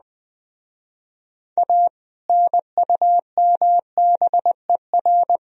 Morse code, heard as a keyed tone, about 23 wpm, 700 Hz. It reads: A NUMBER